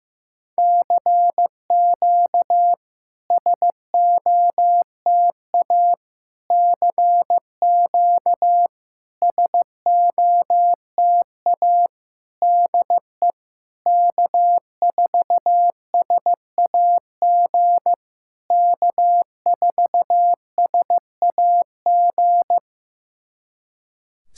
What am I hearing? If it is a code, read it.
Morse code, heard as a keyed tone, about 14 wpm, 700 Hz. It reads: CQ SOTA CQ SOTA DE K4SAG K4SAG
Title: SOTA QSO Practice 15wpm